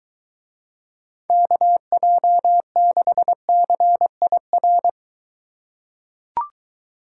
Morse code, heard as a keyed tone, about 23 wpm, 700 Hz. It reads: KJ6CIR